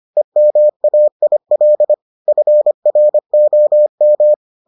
Morse code, heard as a keyed tone, about 25 wpm, 600 Hz. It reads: EMAIL FROM